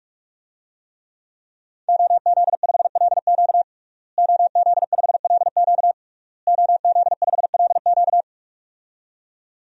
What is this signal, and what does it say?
Morse code, heard as a keyed tone, about 42 wpm, 700 Hz. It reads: OZ5LQ OZ5LQ OZ5LQ
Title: DXCC-100 Call Signs - 5 Characters - Repeated 3x - 45wpm